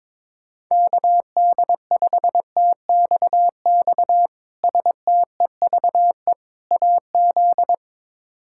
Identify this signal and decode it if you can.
Morse code, heard as a keyed tone, about 22 wpm, 700 Hz. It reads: KD5TXX STEVE AZ
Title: K1USN SST QSO Practice - 22wpm